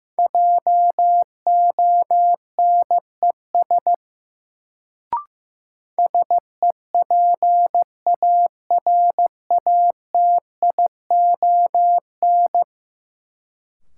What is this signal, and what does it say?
Morse code, heard as a keyed tone, about 15 wpm, 700 Hz. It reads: JONES SEPARATION
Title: Sets of 5 Words - New Word Drill - 15wpm